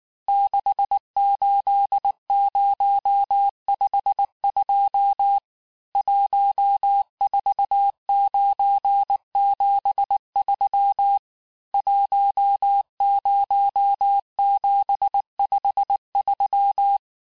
Morse code, heard as a keyed tone, about 19 wpm, 800 Hz. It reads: 68052 14973 10753
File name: cours-du-mercredi / audios / cours-869-texte-CHIFFRE-vitesse-18-MPM